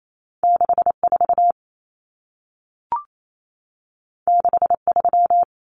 Morse code, heard as a keyed tone, about 28 wpm, 700 Hz. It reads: 64 63
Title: Numbers - 2-Digit 28wpm